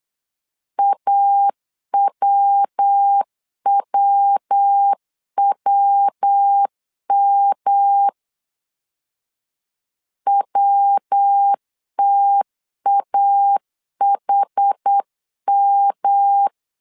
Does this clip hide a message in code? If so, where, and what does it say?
Yes: Morse code from 0.8 to 16.5 s, heard as a keyed tone, about 8 wpm, 800 Hz. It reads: AWWWM WTAHM